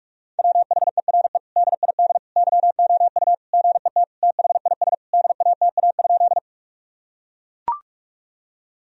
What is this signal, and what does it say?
Morse code, heard as a keyed tone, about 41 wpm, 700 Hz. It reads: WHERE DID YOU GET THIS DATA?
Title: Sentences from Top 1000 Words - Review Part 3 - 45wpm